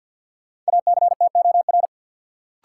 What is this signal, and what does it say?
Morse code, heard as a keyed tone, about 51 wpm, 700 Hz. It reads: ACTOR
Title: Sentences from Top 1500 Words - New Word Drill Alt - 50wpm